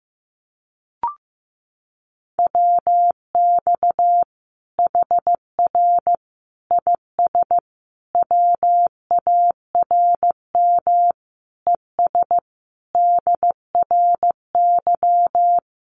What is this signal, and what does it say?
Morse code, heard as a keyed tone, about 15 wpm, 700 Hz. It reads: WX HR IS WARM ES DRY
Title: Common QSO Phrases 15wpm